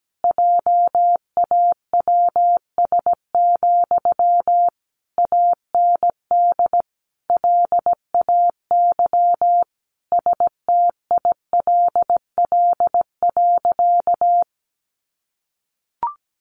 Morse code, heard as a keyed tone, about 17 wpm, 700 Hz. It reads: JAWS, AND LAY STILL.
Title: A Sound Of Thunder 17wpm